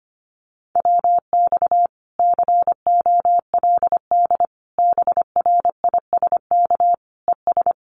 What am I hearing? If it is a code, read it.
Morse code, heard as a keyed tone, about 25 wpm, 700 Hz. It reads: WX COLD BRISK ES
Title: Common QSO Phrases Encore 25wpm